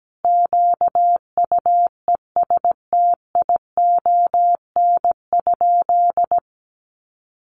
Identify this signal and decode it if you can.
Morse code, heard as a keyed tone, about 17 wpm, 700 Hz. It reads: QUESTION?
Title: Sentences from Top 1100 Words 17wpm